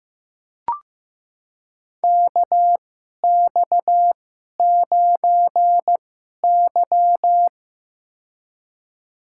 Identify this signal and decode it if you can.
Morse code, heard as a keyed tone, about 15 wpm, spaced as for 11 wpm, 690 Hz. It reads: KX9Y
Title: USA Call Signs - 4 Characters - 2x Character Spacing - 15wpm